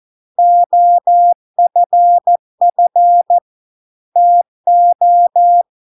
Morse code, heard as a keyed tone, about 14 wpm, 700 Hz. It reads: OFF TO